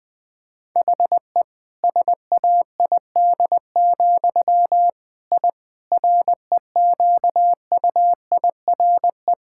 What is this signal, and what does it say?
Morse code, heard as a keyed tone, about 20 wpm, 700 Hz. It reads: HE SAID, I REQUIRE
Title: Sentences from Top 1000 Words - Review Part 6 - 20wpm